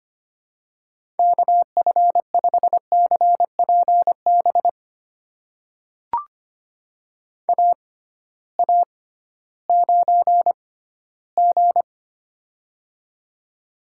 Morse code, heard as a keyed tone, about 25 wpm, spaced as for 15 wpm, 700 Hz. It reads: KF5CPB AA9G
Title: USA Call Signs - 6x Character Spacing - 25wpm